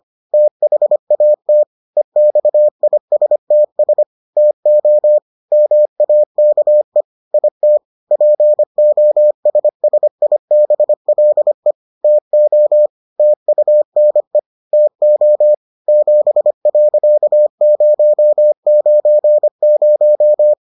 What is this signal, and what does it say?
Morse code, heard as a keyed tone, about 25 wpm, 600 Hz. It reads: THAT EXISTS TO MAKE IT POSSIBLE TO TUNE TO 7.090